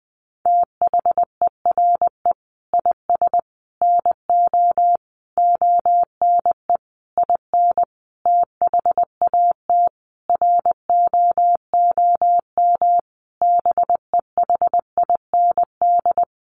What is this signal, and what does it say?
Morse code, heard as a keyed tone, about 20 wpm, 700 Hz. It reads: THERE IS NO ONE IN THAT ROOM BEHIND
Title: Sentences from Top 500 Words 20wpm